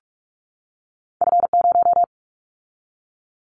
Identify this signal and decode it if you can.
Morse code, heard as a keyed tone, about 44 wpm, 700 Hz. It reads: F0
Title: Callsign Prefixes for Top 25 DXCC - 45wpm